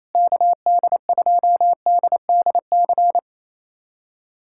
Morse code, heard as a keyed tone, about 28 wpm, 700 Hz. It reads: KD2DDC
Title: USA Callsign DE Callsign 28wpm